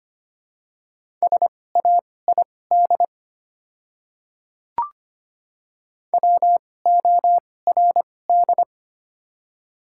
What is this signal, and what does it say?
Morse code, heard as a keyed tone, about 25 wpm, spaced as for 18 wpm, 690 Hz. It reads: SAID WORD
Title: 2x Character Spacing - Top 100 Words - 25wpm